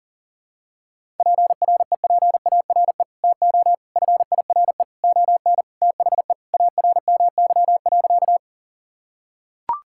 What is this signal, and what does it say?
Morse code, heard as a keyed tone, about 40 wpm, 700 Hz. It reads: PREPARE TO FIRE ON THE ARMY.
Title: Sentences from Top 1000 Words - Review Part 6 - 40wpm